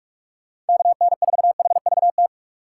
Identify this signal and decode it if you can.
Morse code, heard as a keyed tone, about 57 wpm, 700 Hz. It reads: KN4HVT